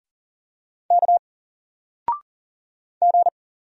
Morse code, heard as a keyed tone, about 40 wpm, 700 Hz. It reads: K G
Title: Single Letter-Number 40wpm